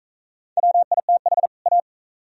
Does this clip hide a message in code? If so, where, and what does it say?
Yes: Morse code from 0.6 to 1.8 s, heard as a keyed tone, about 38 wpm, 700 Hz. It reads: WITH A